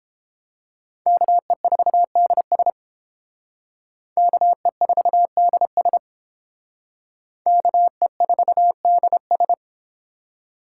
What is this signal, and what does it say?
Morse code, heard as a keyed tone, about 30 wpm, 700 Hz. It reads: KE4DS KE4DS KE4DS